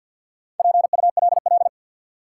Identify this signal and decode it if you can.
Morse code, heard as a keyed tone, about 52 wpm, 700 Hz. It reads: PULL
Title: Sentences from Top 400 Words - New Word Drill - 50wpm